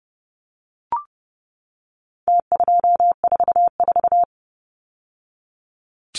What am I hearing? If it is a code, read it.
Morse code, heard as a keyed tone, about 30 wpm, 700 Hz. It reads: T244
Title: Numbers - Contest Style - 4-Digit 30wpm